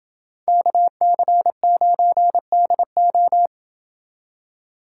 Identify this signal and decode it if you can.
Morse code, heard as a keyed tone, about 28 wpm, 700 Hz. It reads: KC9DO